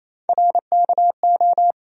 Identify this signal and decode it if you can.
Morse code, heard as a keyed tone, about 28 wpm, 700 Hz. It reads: RKO
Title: USA Call Sign Postfixes - 3 Letters 28wpm